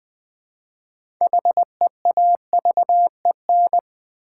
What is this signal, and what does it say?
Morse code, heard as a keyed tone, about 20 wpm, 700 Hz. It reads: HEAVEN
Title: 6 Letter Words - First 3 Letters Given - 20wpm